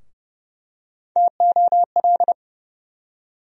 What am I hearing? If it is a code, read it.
Morse code, heard as a keyed tone, about 29 wpm, 700 Hz. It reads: TOL
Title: USA Call Sign Postfixes - 3 Letters 30wpm